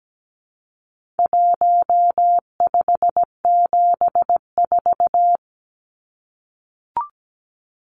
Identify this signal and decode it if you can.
Morse code, heard as a keyed tone, about 17 wpm, 700 Hz. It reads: 1574
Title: Numbers - 4-Digit 17wpm